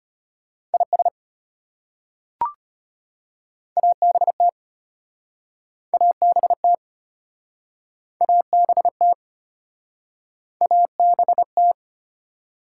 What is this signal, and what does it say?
Morse code, heard as a keyed tone, about 32 wpm, 700 Hz. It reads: IS ABT ABT ABT ABT